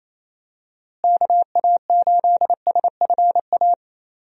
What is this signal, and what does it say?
Morse code, heard as a keyed tone, about 28 wpm, 700 Hz. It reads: KA8SFA
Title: USA Call Signs - 6 Characters - Repeated 2x - 28wpm